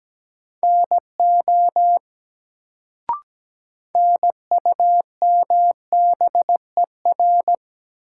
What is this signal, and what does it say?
Morse code, heard as a keyed tone, about 17 wpm, 700 Hz. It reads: NO NUMBER
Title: Sentences from Top 100 Words - New Word Drill - 17wpm